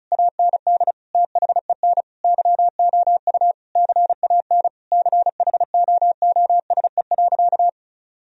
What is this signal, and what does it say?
Morse code, heard as a keyed tone, about 34 wpm, 700 Hz. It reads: AND THEN YOU CAN CHOOSE.
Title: Sentences from Top 1000 Words - Review Part 4 - 35wpm